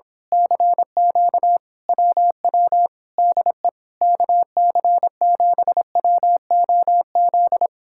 Morse code, heard as a keyed tone, about 26 wpm, 700 Hz. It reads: CQ WW DE KC7WOZ
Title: Speed-Racing Common QSO Phrases Encore - 22wpm